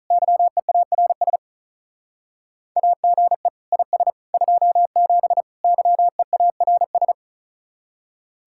Morse code, heard as a keyed tone, about 44 wpm, 700 Hz. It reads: YEARS AGE IS 27 YEARS